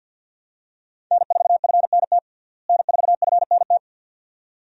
Morse code, heard as a keyed tone, about 46 wpm, 700 Hz. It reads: N4FNT N4FNT
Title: USA Call Signs - 5 Characters - Repeated 2x - 50wpm